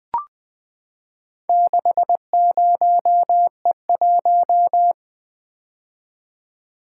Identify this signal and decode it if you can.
Morse code, heard as a keyed tone, about 20 wpm, 700 Hz. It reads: 60E1
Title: Four Letter-Number Combinations - 20wpm